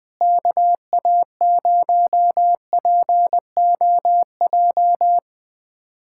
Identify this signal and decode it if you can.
Morse code, heard as a keyed tone, about 20 wpm, 700 Hz. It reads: KA0POJ
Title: USA Callsign DE Callsign - 6 Characters - 20wpm